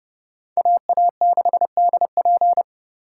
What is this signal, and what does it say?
Morse code, heard as a keyed tone, about 34 wpm, 700 Hz. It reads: AA6DP